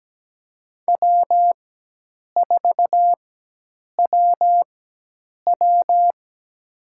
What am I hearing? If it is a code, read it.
Morse code, heard as a keyed tone, about 17 wpm, spaced as for 8 wpm, 700 Hz. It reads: W4WW
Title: USA Call Signs - 4 Characters - 4x Character Spacing - 17wpm